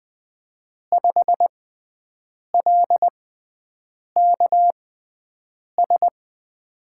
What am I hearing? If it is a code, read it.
Morse code, heard as a keyed tone, about 20 wpm, spaced as for 7 wpm, 700 Hz. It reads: 5LKS
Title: USA Call Signs - 5 Characters - 6x Character Spacing - 20wpm